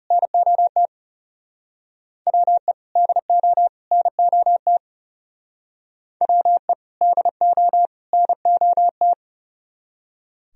Morse code, heard as a keyed tone, about 37 wpm, 700 Hz. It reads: NOT WE DO NOT WE DO NOT
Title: Speed-Racing - Sets of 3 Words 30wpm